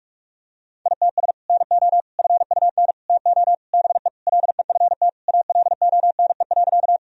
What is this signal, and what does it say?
Morse code, heard as a keyed tone, about 46 wpm, 700 Hz. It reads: ITS NO FUN TO BE LEFT ALONE.
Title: Sentences from Top 1000 Words - Review Part 4 - 45wpm